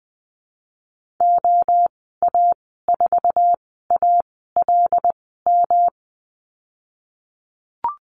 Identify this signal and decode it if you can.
Morse code, heard as a keyed tone, about 20 wpm, spaced as for 14 wpm, 700 Hz. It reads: OA4ALM
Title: 173 Lesson 28g - Number 9 - Call Signs 20wpm